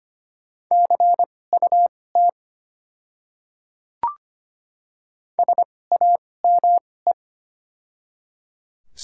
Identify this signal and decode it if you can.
Morse code, heard as a keyed tone, about 25 wpm, spaced as for 18 wpm, 700 Hz. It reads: CUT SAME
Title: 2x Character Spacing - Top 300 Words - 25wpm